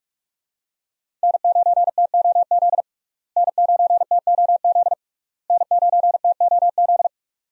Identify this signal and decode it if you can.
Morse code, heard as a keyed tone, about 43 wpm, 700 Hz. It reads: N9TOZ N9TOZ N9TOZ
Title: USA Call Signs - 5 Characters - Repeated 3x - 45wpm